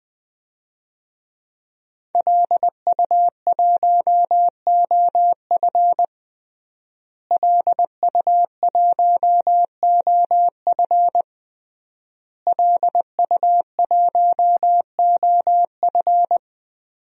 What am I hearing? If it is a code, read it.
Morse code, heard as a keyed tone, about 21 wpm, 700 Hz. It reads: LU1OF LU1OF LU1OF
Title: DXCC-100 Call Signs - 5 Characters - Repeated 3x - 20wpm